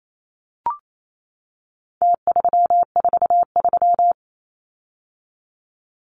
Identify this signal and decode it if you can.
Morse code, heard as a keyed tone, about 28 wpm, 700 Hz. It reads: T343
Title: Numbers - Contest Style - 4-Digit 28wpm